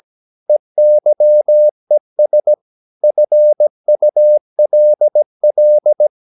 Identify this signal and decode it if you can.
Morse code, heard as a keyed tone, about 17 wpm, 600 Hz. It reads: EYES FULL
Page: Romanen Never af Ken Follet på engelsk.